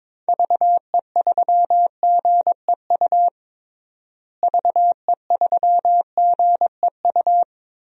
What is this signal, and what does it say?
Morse code, heard as a keyed tone, about 22 wpm, 700 Hz. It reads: VE3GEU VE3GEU
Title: DXCC-100 Call Signs - 6 Characters - Repeated 3x - 22wpm